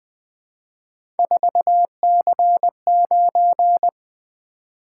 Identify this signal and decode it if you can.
Morse code, heard as a keyed tone, about 20 wpm, 700 Hz. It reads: 4C9
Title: Three Letter-Number Combinations - 20wpm